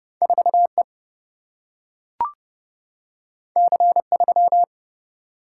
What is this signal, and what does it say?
Morse code, heard as a keyed tone, about 31 wpm, 700 Hz. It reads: 4 E C 3
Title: Single-Letter Single-Number 30wpm